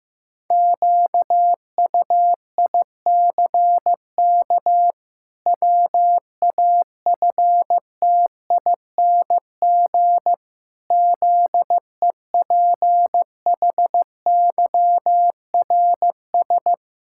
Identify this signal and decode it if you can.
Morse code, heard as a keyed tone, about 16 wpm, 700 Hz. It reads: QUICK WAFTING ZEPHYRS
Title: Pangram Challenge v2 - 15wpm